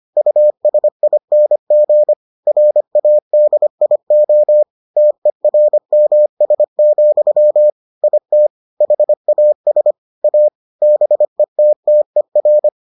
Morse code, heard as a keyed tone, about 25 wpm, 600 Hz. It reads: USING RADIO TERMS, IT HAS A BETTER